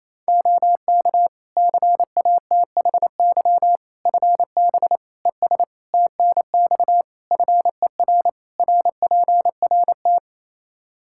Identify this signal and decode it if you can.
Morse code, heard as a keyed tone, about 28 wpm, 700 Hz. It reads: OK CATHY FB ES TNX FER RPRT
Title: K1USN SST QSO Practice 28wpm